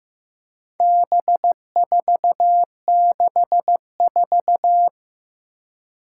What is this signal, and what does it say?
Morse code, heard as a keyed tone, about 15 wpm, 700 Hz. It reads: B464
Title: Four Challenging Letter-Number Combinations 15wpm